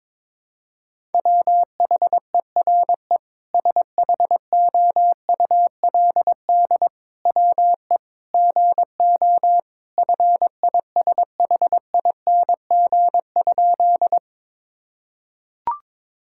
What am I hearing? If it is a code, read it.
Morse code, heard as a keyed tone, about 22 wpm, 700 Hz. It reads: WHERE SHOULD WE GO FISHING?
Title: Sentences from Top 1000 Words - Review Part 5 - 22wpm